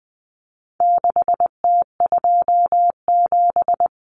Morse code, heard as a keyed tone, about 20 wpm, 700 Hz. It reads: 6T27
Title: Common Cut Numbers - 4-Digit 20wpm